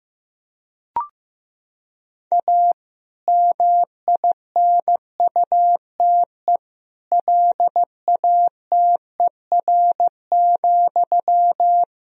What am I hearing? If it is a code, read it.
Morse code, heard as a keyed tone, about 15 wpm, 700 Hz. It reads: A MINUTE LATER,
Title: Sentences from Top 1000 Words - Review Part 4 - 15wpm